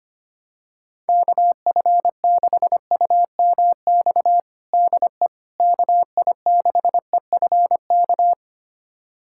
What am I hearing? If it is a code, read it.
Morse code, heard as a keyed tone, about 25 wpm, 700 Hz. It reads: KF6UMX DE KI6EFK
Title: USA Callsign DE Callsign - 6 Characters - 25wpm